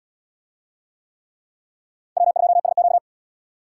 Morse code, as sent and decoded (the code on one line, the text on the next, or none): .-- ..--- ... --.-
W2SQ